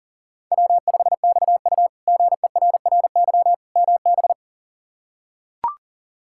Morse code, heard as a keyed tone, about 37 wpm, 700 Hz. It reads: W5XU GERRY MB
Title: K1USN SST QSO Practice - 40wpm